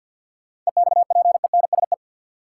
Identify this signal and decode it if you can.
Morse code, heard as a keyed tone, about 52 wpm, 700 Hz. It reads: EXPENSE
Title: Sets of 4 Words - New Word Drill - 50wpm